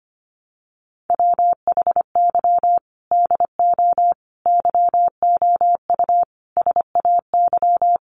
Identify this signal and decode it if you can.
Morse code, heard as a keyed tone, about 25 wpm, 700 Hz. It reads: WHY DO YOU SAY
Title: Sentences from Top 1100 Words 25wpm